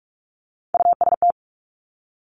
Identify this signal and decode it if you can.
Morse code, heard as a keyed tone, about 46 wpm, 700 Hz. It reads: UST
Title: Top Three Letter Combinations in Words 45wpm